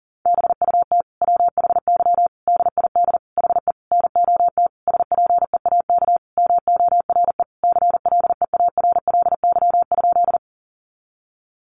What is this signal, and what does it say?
Morse code, heard as a keyed tone, about 40 wpm, 700 Hz. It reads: BUT WHY DID HE NOT SPEAK MORE CLEARLY?
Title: Sentences from Top 1300 Words 40wpm